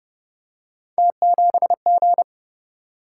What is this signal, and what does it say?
Morse code, heard as a keyed tone, about 26 wpm, 700 Hz. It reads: T7G